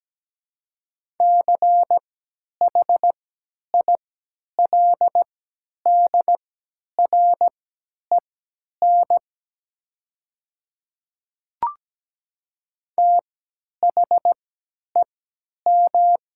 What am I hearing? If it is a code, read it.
Morse code, heard as a keyed tone, about 17 wpm, spaced as for 10 wpm, 700 Hz. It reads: CHILDREN THEM
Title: 3x Character Spacing - Top 300 Words - 17wpm